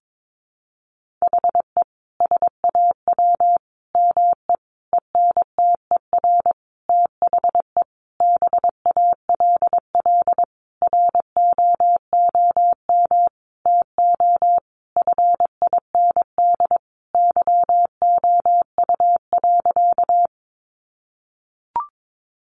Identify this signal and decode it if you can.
Morse code, heard as a keyed tone, about 22 wpm, 700 Hz. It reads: HE SAW ME ENTER THE BALL ROOM TO FIND YOU.
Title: Sentences from Top 1300 Words 22wpm